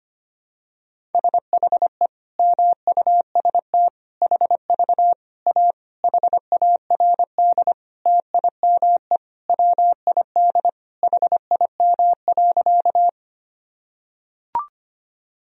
Morse code, heard as a keyed tone, about 25 wpm, 700 Hz. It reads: SHE MUST HV A HARD TIME WID HIM.
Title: Sentences from Top 300 Words and Abbreviations 25wpm